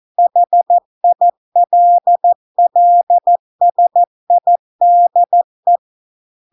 Morse code, heard as a keyed tone, about 14 wpm, 700 Hz. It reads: HILLSIDE